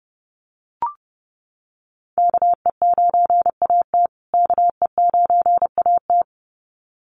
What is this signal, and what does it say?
Morse code, heard as a keyed tone, about 30 wpm, 700 Hz. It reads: KE9AT KE9AT
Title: USA Call Signs Repeated 2x 30wpm